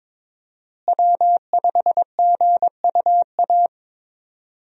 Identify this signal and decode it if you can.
Morse code, heard as a keyed tone, about 22 wpm, 700 Hz. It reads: W5GUA
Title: USA Call Signs - 5 Characters - Repeated 2x - 22wpm